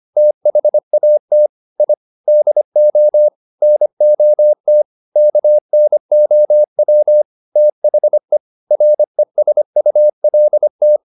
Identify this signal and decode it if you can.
Morse code, heard as a keyed tone, about 25 wpm, 600 Hz. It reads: THAT I DO NOT KNOW THE RESULT
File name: the old man and the sea part 1 25wpm